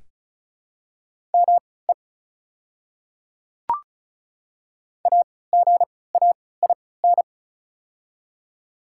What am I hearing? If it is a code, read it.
Morse code, heard as a keyed tone, about 36 wpm, spaced as for 20 wpm, 700 Hz. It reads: ME AGAIN
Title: 3x Character Spacing - Top 200 Words - 35wpm